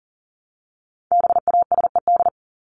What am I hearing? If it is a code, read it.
Morse code, heard as a keyed tone, about 40 wpm, 700 Hz. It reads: BASED
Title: ED Suffix - Top 100 Words with Base Words 40wpm